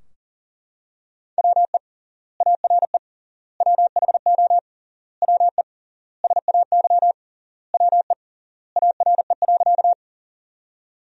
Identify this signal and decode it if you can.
Morse code, heard as a keyed tone, about 37 wpm, 700 Hz. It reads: WE ARE WHO WE SAY WE ARE.
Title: 3x Word-Spacing - Sentences from Top 200 Words - 40wpm